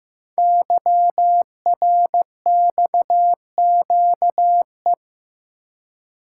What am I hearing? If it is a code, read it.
Morse code, heard as a keyed tone, about 15 wpm, 700 Hz. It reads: YRXQE
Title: Code Groups - 5 Letters - 15wpm